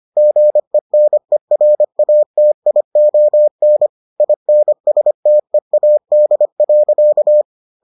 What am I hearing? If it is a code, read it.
Morse code, heard as a keyed tone, about 25 wpm, 600 Hz. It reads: GENERATION INSTEAD.